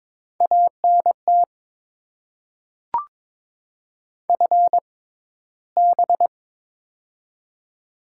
Morse code, heard as a keyed tone, about 22 wpm, spaced as for 14 wpm, 690 Hz. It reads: ANT FB
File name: Top 20 QSO Elements - 6x Character Spacing - 22wpm